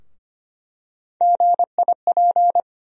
Morse code, heard as a keyed tone, about 25 wpm, 700 Hz. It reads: GIP
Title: USA Call Sign Postfixes - 3 Letters 25wpm